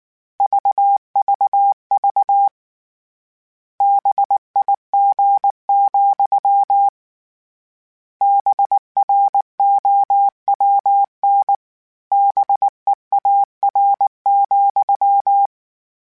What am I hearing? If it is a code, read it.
Morse code, heard as a keyed tone, about 19 wpm, 800 Hz. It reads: VVV BIG, BROWN BEAR,
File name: Misc / Misc-mp3s / BIG @18